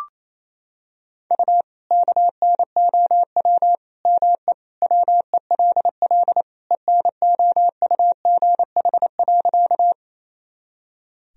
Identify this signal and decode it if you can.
Morse code, heard as a keyed tone, about 28 wpm, 700 Hz. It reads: U KNOW ME WELL ENOUGH.
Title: Sentences from Top 400 Words and Abbreviations 28wpm